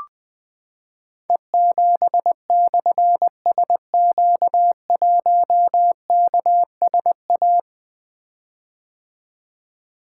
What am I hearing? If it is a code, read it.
Morse code, heard as a keyed tone, about 20 wpm, 700 Hz. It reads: E7/SQ1KSA
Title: Difficult Call Signs 20wpm